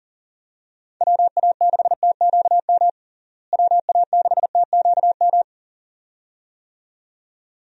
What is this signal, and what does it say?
Morse code, heard as a keyed tone, about 39 wpm, 700 Hz. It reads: WA6TQM WA6TQM
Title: USA Call Signs - 6 Characters - Repeated 2x - 40wpm